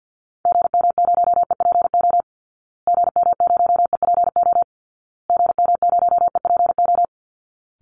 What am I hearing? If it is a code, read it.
Morse code, heard as a keyed tone, about 51 wpm, 700 Hz. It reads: GM0EPO GM0EPO GM0EPO
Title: Call Signs Repeated 2x - 50wpm